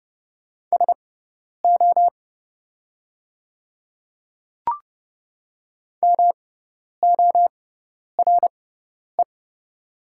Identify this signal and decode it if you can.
Morse code, heard as a keyed tone, about 30 wpm, spaced as for 10 wpm, 700 Hz. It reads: SO MORE
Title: 6x Character Spacing - Top 100 Words - 30wpm